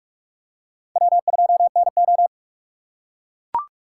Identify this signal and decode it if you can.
Morse code, heard as a keyed tone, about 42 wpm, 700 Hz. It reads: W2NO